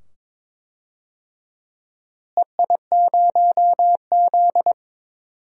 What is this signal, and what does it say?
Morse code, heard as a keyed tone, about 21 wpm, 700 Hz. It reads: EI0Z
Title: DXCC-100 Call Signs - 4 Characters - Repeated 2x - 22wpm